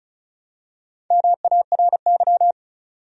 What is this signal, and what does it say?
Morse code, heard as a keyed tone, about 34 wpm, 700 Hz. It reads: MARY
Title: USA Common Names - Max 4 Letters - 35wpm